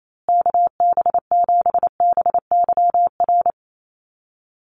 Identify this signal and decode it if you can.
Morse code, heard as a keyed tone, about 28 wpm, 700 Hz. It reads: KB7BYR
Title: USA Call Signs Repeated 2x 28wpm